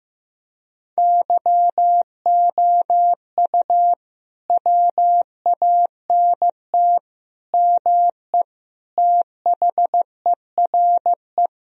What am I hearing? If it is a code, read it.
Morse code, heard as a keyed tone, about 15 wpm, 700 Hz. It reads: YOU WANT ME THERE
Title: Sentences from Top 1000 Words - Review Part 2 - 15wpm